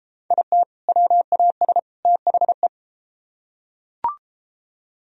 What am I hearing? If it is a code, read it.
Morse code, heard as a keyed tone, about 32 wpm, 700 Hz. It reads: IT WAS THE